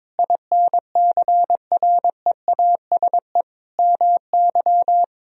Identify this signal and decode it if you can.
Morse code, heard as a keyed tone, about 22 wpm, 700 Hz. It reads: INCREASE MY
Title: Sentences from Top 1000 Words - Review Part 4 - 22wpm